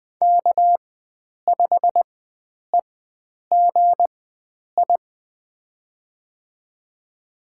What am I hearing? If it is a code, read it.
Morse code, heard as a keyed tone, about 20 wpm, spaced as for 9 wpm, 700 Hz. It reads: K5EGI
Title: USA Call Signs - 4x Character Spacing - 20wpm